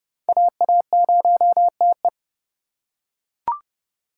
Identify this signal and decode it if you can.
Morse code, heard as a keyed tone, about 31 wpm, 700 Hz. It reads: AA0TE